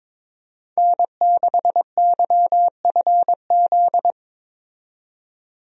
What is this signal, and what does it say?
Morse code, heard as a keyed tone, about 22 wpm, 700 Hz. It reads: N6YFZ
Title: USA Call Signs - 5 Characters - 22wpm